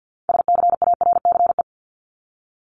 Morse code, heard as a keyed tone, about 50 wpm, 700 Hz. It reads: SCARCE
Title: LY Suffix - Top 100 Words with Base Words 50wpm